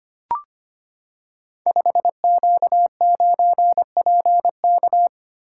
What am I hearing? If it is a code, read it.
Morse code, heard as a keyed tone, about 25 wpm, 700 Hz. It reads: 5Q9PK
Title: Code Groups - 5 Letter:Numbers - 25wpm